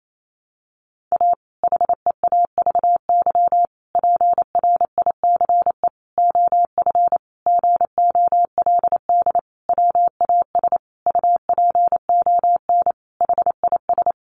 Morse code, heard as a keyed tone, about 28 wpm, 700 Hz. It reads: A HEAVY PRICE OF GOLD WAS UPON HIS
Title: Sentences from Top 1100 Words 28wpm